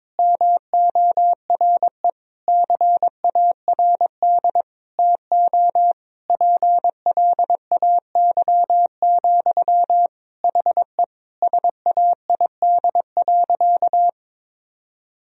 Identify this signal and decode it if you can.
Morse code, heard as a keyed tone, about 22 wpm, 700 Hz. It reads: MORE CARD TO PLAY, HE SAID.
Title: Sentences from Top 1000 Words - Review Part 4 - 22wpm